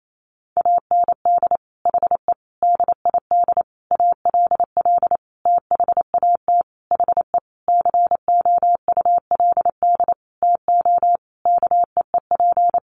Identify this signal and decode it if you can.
Morse code, heard as a keyed tone, about 28 wpm, 700 Hz. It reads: AND HE DID ALL THAT HE COULD TO KEEP
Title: Sentences from Top 1500 Words 28wpm